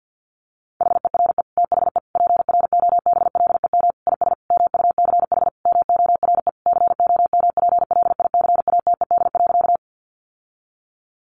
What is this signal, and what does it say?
Morse code, heard as a keyed tone, about 50 wpm, 700 Hz. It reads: HERE THE PROBLEM IS MUCH MORE COMPLICATED.
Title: Sentences from Top 1200 Words 50wpm